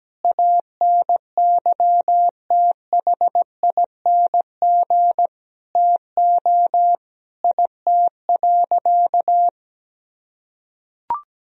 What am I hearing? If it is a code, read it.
Morse code, heard as a keyed tone, about 17 wpm, 700 Hz. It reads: ANYTHING TO IT.
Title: Sentences from Top 1000 Words - Review Part 1 - 17wpm